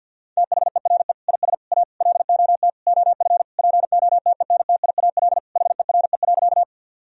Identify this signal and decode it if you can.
Morse code, heard as a keyed tone, about 48 wpm, 700 Hz. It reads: THERE IS A LOT OF POTENTIAL HERE.
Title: Sentences from Top 1000 Words - Review Part 4 - 50wpm